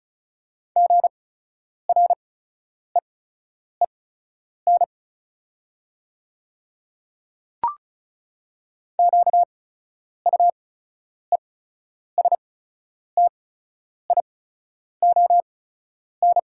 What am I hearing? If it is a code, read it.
Morse code, heard as a keyed tone, about 36 wpm, spaced as for 10 wpm, 700 Hz. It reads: GREEN QUESTION
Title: 8x Character Spacing - Top 500 Words - 35wpm